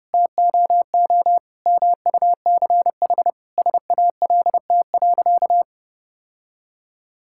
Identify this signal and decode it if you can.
Morse code, heard as a keyed tone, about 30 wpm, 700 Hz. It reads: TOO MUCH SALT.
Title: Sentences from Top 1000 Words - Review Part 5 - 30wpm